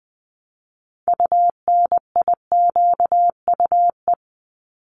Morse code, heard as a keyed tone, about 20 wpm, 700 Hz. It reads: UNIQUE
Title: Top 751-1000 Words 20wpm